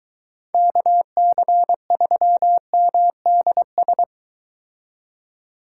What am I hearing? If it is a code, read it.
Morse code, heard as a keyed tone, about 22 wpm, 700 Hz. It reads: KC3MDS